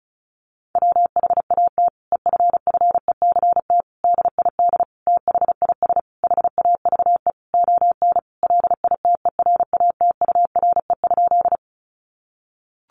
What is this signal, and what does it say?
Morse code, heard as a keyed tone, about 34 wpm, 700 Hz. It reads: WHAT EFFECT DID THIS HAVE ON LITERATURE?
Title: Sentences from Top 1000 Words 35wpm